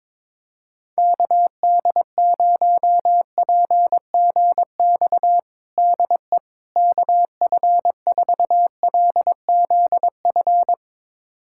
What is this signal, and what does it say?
Morse code, heard as a keyed tone, about 22 wpm, 700 Hz. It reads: KD0PGX DE KF4LZF
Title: USA Callsign DE Callsign 22wpm